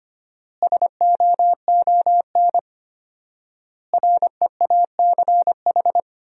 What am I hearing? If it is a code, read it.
Morse code, heard as a keyed tone, about 25 wpm, 700 Hz. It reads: SOON REACH
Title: 4x Word-Spacing - Sentences from Top 400 Words - 25wpm